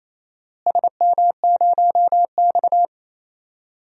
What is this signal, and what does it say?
Morse code, heard as a keyed tone, about 27 wpm, 700 Hz. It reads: SM0X
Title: DXCC-100 Call Signs - 4 Characters - Repeated 3x - 28wpm